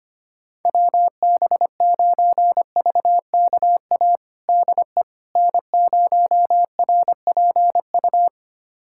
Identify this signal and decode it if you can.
Morse code, heard as a keyed tone, about 25 wpm, 700 Hz. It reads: WB9VKA DE N0RPU
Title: USA Callsign DE Callsign 25wpm